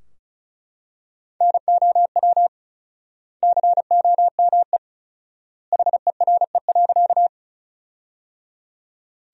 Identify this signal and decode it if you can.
Morse code, heard as a keyed tone, about 36 wpm, 700 Hz. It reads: NOW COME HERE.
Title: 4x Word-Spacing - Sentences from Top 200 Words - 35wpm